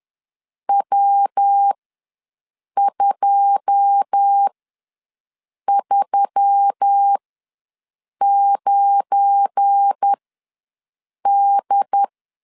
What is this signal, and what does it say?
Morse code, heard as a keyed tone, about 10 wpm, spaced as for 6 wpm, 800 Hz. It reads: W239D
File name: LEZIONI / 9 2 ws 10wpm 4130